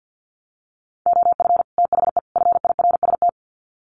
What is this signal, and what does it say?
Morse code, heard as a keyed tone, about 49 wpm, 700 Hz. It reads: OF THE FIRST
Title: Sets of 3 Words 50wpm